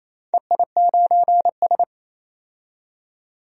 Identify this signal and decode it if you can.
Morse code, heard as a keyed tone, about 28 wpm, 700 Hz. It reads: EI9S
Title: DXCC-100 Call Signs - 4 Characters - 28wpm